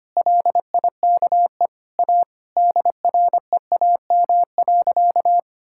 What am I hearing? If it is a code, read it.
Morse code, heard as a keyed tone, about 25 wpm, 700 Hz. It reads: LIKE A DREAM.
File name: practice / 25wpm / Sentences with 2-Word Phrases - 25wpm